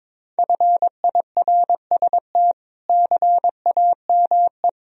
Morse code, heard as a keyed tone, about 22 wpm, 700 Hz. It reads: FIRST CAME
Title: Sentences from Top 1000 Words - Review Part 1 - 22wpm